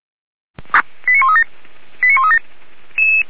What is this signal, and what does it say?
Five-tone selective call (ZVEI), heard as quick runs of five tones, tones of about 70 ms. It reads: 89137 89137